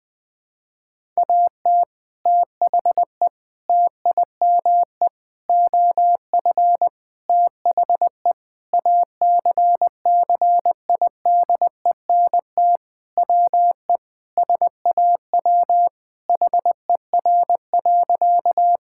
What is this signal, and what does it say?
Morse code, heard as a keyed tone, about 21 wpm, 700 Hz. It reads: AT THE TIME OF THE ACCIDENT WE SAW HER.
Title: Sentences with 4-Word Phrases - 20wpm